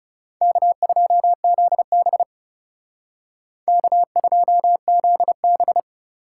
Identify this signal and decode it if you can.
Morse code, heard as a keyed tone, about 36 wpm, 700 Hz. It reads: K2ZB K2ZB
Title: Speed Racing - USA 1x2 Call Signs 30wpm